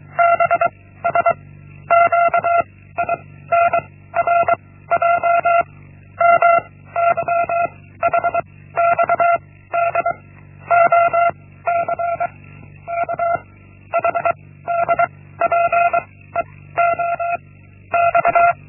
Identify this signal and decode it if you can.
Morse code, heard as a keyed tone, about 23 wpm, 690 Hz. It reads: B S Q I N R J M Y H X D O C K H D P E O X
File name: Linksmakalnis-1